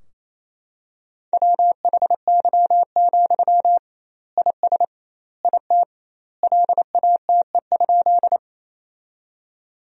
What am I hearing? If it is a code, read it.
Morse code, heard as a keyed tone, about 28 wpm, 700 Hz. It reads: WHY, IS IT LATE?
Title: 2x Word-Spacing - Sentences from Top 300 Words - 28wpm